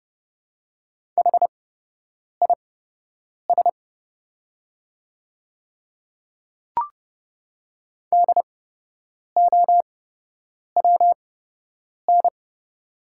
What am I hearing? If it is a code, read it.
Morse code, heard as a keyed tone, about 30 wpm, spaced as for 8 wpm, 700 Hz. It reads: HIS DOWN
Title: 8x Character Spacing - Top 100 Words - 30wpm